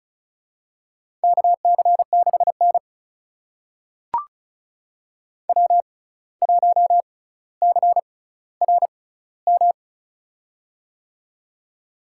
Morse code, heard as a keyed tone, about 34 wpm, spaced as for 17 wpm, 700 Hz. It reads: KC6N W1CRM
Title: USA Call Signs - 6x Character Spacing - 35wpm